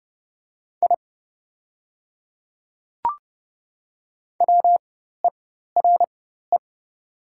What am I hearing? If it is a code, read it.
Morse code, heard as a keyed tone, about 30 wpm, spaced as for 14 wpm, 700 Hz. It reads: I WERE
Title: 4x Character Spacing - Top 200 Words - 30wpm